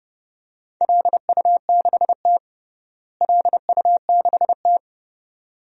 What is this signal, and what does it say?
Morse code, heard as a keyed tone, about 32 wpm, 700 Hz. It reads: LU6T LU6T
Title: DXCC-100 Call Signs - 4 Characters - Repeated 2x - 30wpm